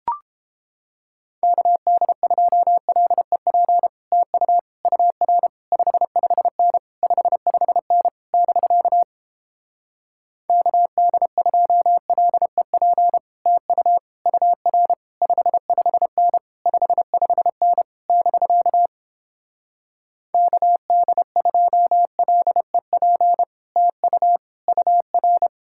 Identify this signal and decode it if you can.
Morse code, heard as a keyed tone, about 29 wpm, 700 Hz. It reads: KD2LEP TU UR 55N 55N <BK> KD2LEP TU UR 55N 55N <BK> KD2LEP TU UR